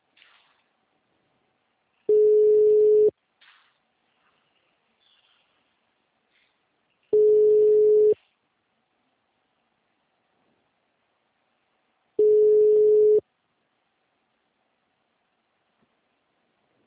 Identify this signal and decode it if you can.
Ringback tone (Europe, 425 Hz).
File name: out- -2004- - - .4090